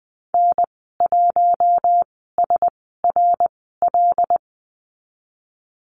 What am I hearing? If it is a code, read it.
Morse code, heard as a keyed tone, about 20 wpm, spaced as for 15 wpm, 700 Hz. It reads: N1SRL
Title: 082 Lesson 14g - Number 2 - Call Signs 20wpm